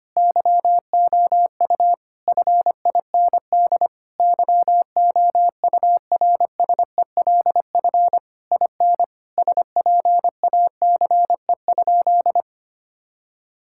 Morse code, heard as a keyed tone, about 25 wpm, 700 Hz. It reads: YOU FIND YOURSELF IN SPACE?
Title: Sentences from Top 1000 Words - Review Part 5 - 25wpm